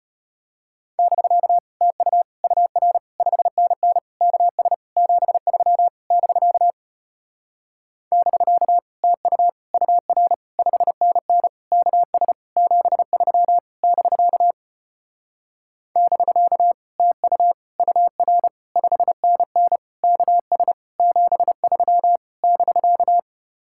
Morse code, heard as a keyed tone, about 33 wpm, 700 Hz. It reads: <BK> TU UR 5NN KS 73 <BK> <BK> TU UR 5NN KS 73 <BK> <BK> TU UR 5NN KS 73 <BK>